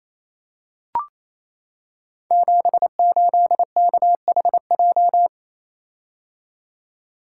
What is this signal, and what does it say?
Morse code, heard as a keyed tone, about 28 wpm, 700 Hz. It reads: 78KHJ
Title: Code Groups - 5 Letter:Numbers - 28wpm